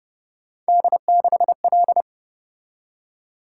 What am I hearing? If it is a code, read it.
Morse code, heard as a keyed tone, about 29 wpm, 700 Hz. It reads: D6L